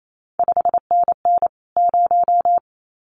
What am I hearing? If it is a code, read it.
Morse code, heard as a keyed tone, about 28 wpm, 700 Hz. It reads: 5NN 0
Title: CQ WW DX Contest Exchange Encore - 28wpm